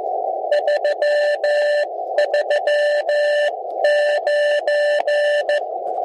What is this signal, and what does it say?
Morse code, heard as a keyed tone, about 15 wpm, spaced as for 13 wpm, 590 Hz, with radio noise. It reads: 339
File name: ABI-SON-QSO-1903z- -7020r0